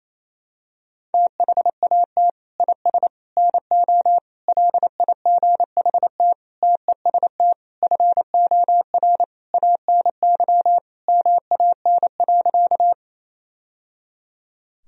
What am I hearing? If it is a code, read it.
Morse code, heard as a keyed tone, about 28 wpm, 700 Hz. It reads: THAT IS NO LIGHT TEST FOR ANY MAN.
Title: Sentences from Top 1000 Words - Review Part 3 - 28wpm